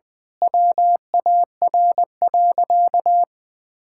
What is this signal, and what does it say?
Morse code, heard as a keyed tone, about 20 wpm, 700 Hz. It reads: WAR.
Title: Sentences from Top 1000 Words - Review Part 6 - 20wpm